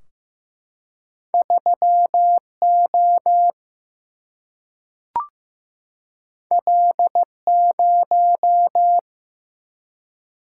Morse code, heard as a keyed tone, about 15 wpm, 700 Hz. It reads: 3O L0
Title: Single-Letter Single-Number 15wpm